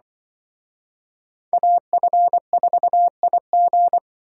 Morse code, heard as a keyed tone, about 23 wpm, 700 Hz. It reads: AF4IG
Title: Speed Racing - USA Call Signs 20wpm